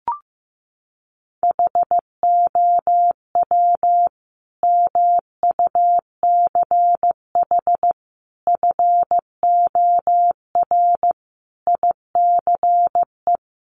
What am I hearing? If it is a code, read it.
Morse code, heard as a keyed tone, about 15 wpm, 700 Hz. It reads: HOW MUCH FOR ICE
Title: Sentences from Top 1500 Words 15wpm